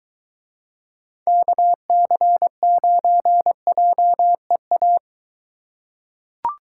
Morse code, heard as a keyed tone, about 23 wpm, 700 Hz. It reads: KC9JEA